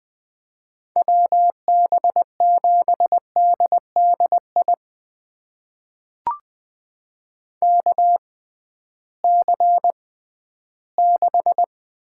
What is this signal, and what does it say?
Morse code, heard as a keyed tone, about 20 wpm, spaced as for 13 wpm, 700 Hz. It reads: WB7DDI KC6
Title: USA Call Signs - 6 Characters - 6x Character Spacing - 20wpm